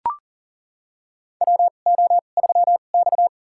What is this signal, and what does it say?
Morse code, heard as a keyed tone, about 39 wpm, spaced as for 29 wpm, 700 Hz. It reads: WO3X
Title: USA Call Signs - 4 Characters - 2x Character Spacing - 40wpm